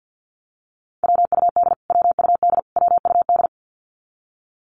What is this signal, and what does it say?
Morse code, heard as a keyed tone, about 48 wpm, 700 Hz. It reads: WUD WUD WUD
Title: Abbreviations Repeated 3x - 50wpm